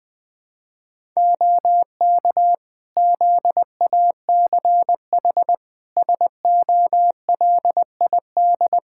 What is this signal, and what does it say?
Morse code, heard as a keyed tone, about 20 wpm, 700 Hz. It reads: OK ZACH SOLID
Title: K1USN SST QSO Practice 20wpm